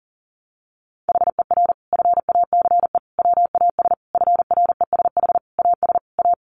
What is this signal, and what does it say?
Morse code, heard as a keyed tone, about 40 wpm, 700 Hz. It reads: HER FACE WAS FRESH AS A
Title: Sentences from Top 1200 Words 40wpm